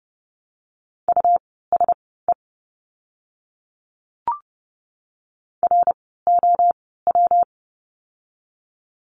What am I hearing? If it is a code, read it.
Morse code, heard as a keyed tone, about 30 wpm, spaced as for 17 wpm, 700 Hz. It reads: USE ROW
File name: course / 30:10wpm / 115 Lesson 19f - Letter w - Words 30wpm